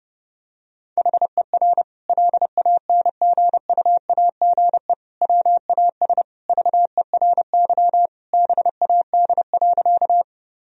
Morse code, heard as a keyed tone, about 30 wpm, 700 Hz. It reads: HER LANGUAGE WAS VERY BAD.
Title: Sentences from Top 1000 Words - Review Part 3 - 30wpm